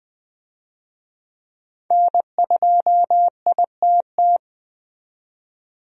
Morse code, heard as a keyed tone, about 20 wpm, 700 Hz. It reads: N2ITT
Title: USA Call Signs - 5 Characters - 20wpm